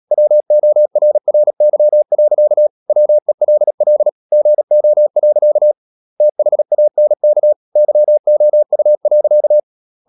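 Morse code, heard as a keyed tone, about 37 wpm, 600 Hz. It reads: WORRY. WELL GO. THANK YOU.
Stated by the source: Morse practice at 36 wpm, text of Ken Follett's novel Never, from CW med Gnister.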